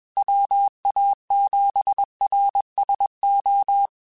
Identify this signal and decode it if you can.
Morse code, heard as a keyed tone, about 21 wpm, 800 Hz. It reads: WA7RSO
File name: QSOs / QSOs-mp3s / kc7haa@20